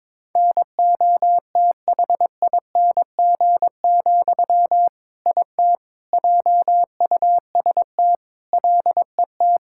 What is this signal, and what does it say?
Morse code, heard as a keyed tone, about 22 wpm, 700 Hz. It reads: NOTHING, IT JUST LET
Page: morse-code-ninja-practice-22wpm